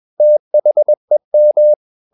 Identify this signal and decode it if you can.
Morse code, heard as a keyed tone, about 21 wpm, 600 Hz. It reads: THEM
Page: CW med Gnister